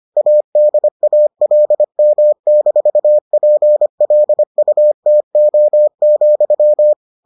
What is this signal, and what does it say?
Morse code, heard as a keyed tone, about 25 wpm, 600 Hz. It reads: ADALM-PLUTO,